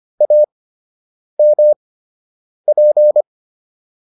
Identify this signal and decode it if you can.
Morse code, heard as a keyed tone, about 25 wpm, spaced as for 8 wpm, 600 Hz. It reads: AMP